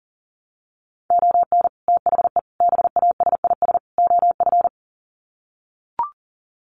Morse code, heard as a keyed tone, about 37 wpm, 700 Hz. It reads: ON THE BASIS OF
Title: Sets of 4 Words 40wpm